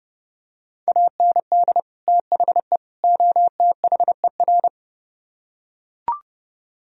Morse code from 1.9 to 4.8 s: THE OTHER

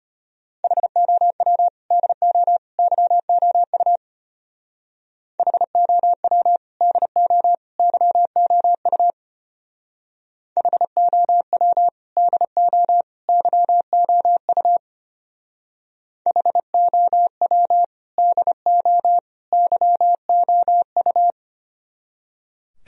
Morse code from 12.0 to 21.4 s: DO YOU HOW DO YOU

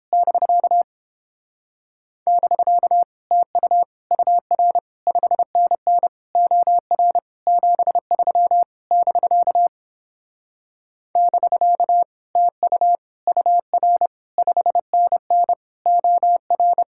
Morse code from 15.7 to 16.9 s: OR

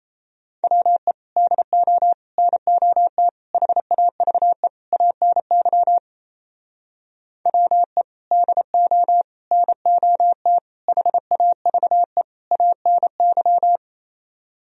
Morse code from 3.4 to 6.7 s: HAVE ANY

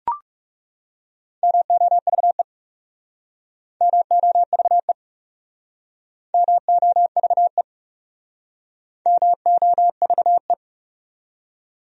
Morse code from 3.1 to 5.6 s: MOVE

